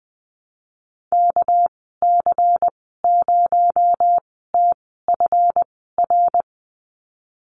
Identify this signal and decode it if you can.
Morse code, heard as a keyed tone, about 20 wpm, spaced as for 15 wpm, 700 Hz. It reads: KC0TFR
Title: 245 Lesson 39g - Letter z - Call Signs 20wpm